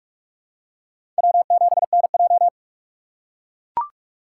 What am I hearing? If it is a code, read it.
Morse code, heard as a keyed tone, about 42 wpm, 700 Hz. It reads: W7NJ
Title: USA Call Signs - 6x Character Spacing - 45wpm